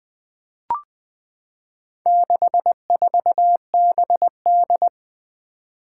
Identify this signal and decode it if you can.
Morse code, heard as a keyed tone, about 20 wpm, 700 Hz. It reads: 64BD
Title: Four Challenging Letter-Number Combinations 20wpm